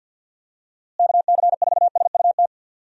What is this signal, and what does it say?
Morse code, heard as a keyed tone, about 52 wpm, 700 Hz. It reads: KC4SUT
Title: USA Call Signs - 6 Characters - Repeated 2x - 50wpm